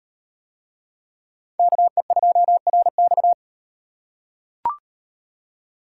Morse code, heard as a keyed tone, about 37 wpm, 700 Hz. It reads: KE2RX